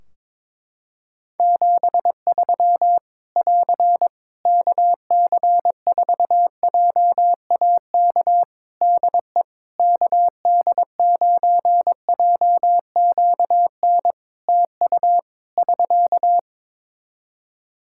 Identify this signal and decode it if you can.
Morse code, heard as a keyed tone, about 22 wpm, 700 Hz. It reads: 73 <AR> KC4JAK DE KD9JQN TU <SK>
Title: Practice QSO - Exchange 4 - Ending - 22wpm